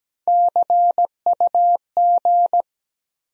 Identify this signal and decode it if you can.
Morse code, heard as a keyed tone, about 17 wpm, 700 Hz. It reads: CUG
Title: USA Call Sign Postfixes - 3 Letters 17wpm